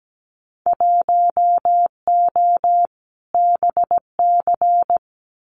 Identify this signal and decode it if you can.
Morse code, heard as a keyed tone, about 17 wpm, 700 Hz. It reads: 1O BC
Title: Winter Field Day Exchanges Encore 17wpm